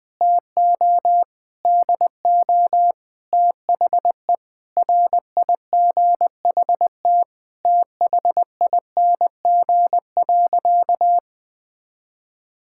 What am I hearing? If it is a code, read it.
Morse code, heard as a keyed tone, about 20 wpm, 700 Hz. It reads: TO DO THE RIGHT THING.
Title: Sentences with 5-Word Phrases - 20wpm